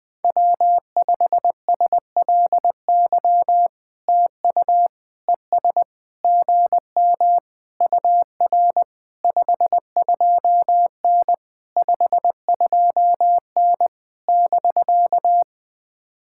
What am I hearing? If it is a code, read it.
Morse code, heard as a keyed tone, about 20 wpm, 700 Hz. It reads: W5SLY TU ES GM UR 52N 52N <BK>
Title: POTA QSO Practice - 20wpm